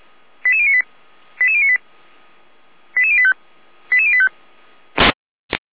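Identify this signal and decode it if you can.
Five-tone selective call (ZVEI), heard as quick runs of five tones, tones of about 80 ms. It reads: 88098 88098 88085 88085
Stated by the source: ZVEI five-tone selective call 88571 received on SDR.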